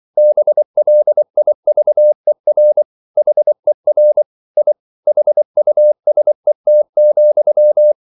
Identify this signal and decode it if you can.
Morse code, heard as a keyed tone, about 24 wpm, 600 Hz. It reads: BLIVER HER I HUSET,
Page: Ferie 24 wpm